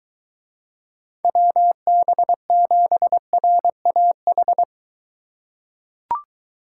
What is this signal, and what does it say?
Morse code, heard as a keyed tone, about 23 wpm, 700 Hz. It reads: WB7RAH